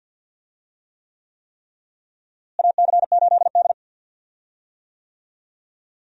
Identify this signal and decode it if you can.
Morse code, heard as a keyed tone, about 51 wpm, 700 Hz. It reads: AC8D
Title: USA Call Signs - 4 Characters - Repeated 3x - 50wpm